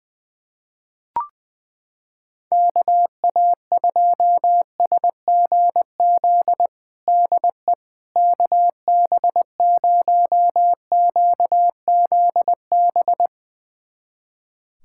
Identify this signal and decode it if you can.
Morse code, heard as a keyed tone, about 20 wpm, 700 Hz. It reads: KA2SGZ DE KB0QZB
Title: USA Callsign DE Callsign 20wpm